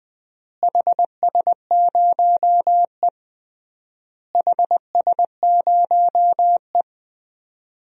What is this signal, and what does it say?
Morse code, heard as a keyed tone, about 21 wpm, 700 Hz. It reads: HS0E HS0E
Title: DXCC-100 Call Signs - 4 Characters - Repeated 3x - 20wpm